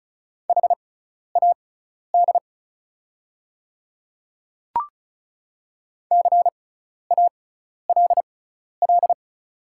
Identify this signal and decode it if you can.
Morse code, heard as a keyed tone, about 34 wpm, spaced as for 12 wpm, 690 Hz. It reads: HAD CALL
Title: 6x Character Spacing - Top 500 Words - 35wpm